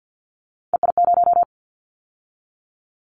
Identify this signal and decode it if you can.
Morse code, heard as a keyed tone, about 50 wpm, 700 Hz. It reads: EI0
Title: Callsign Prefixes for Top 50 DXCC - 50wpm